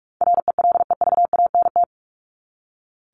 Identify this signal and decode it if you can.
Morse code, heard as a keyed tone, about 44 wpm, 700 Hz. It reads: RELEVANT
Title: Top 1001-1250 Words 45wpm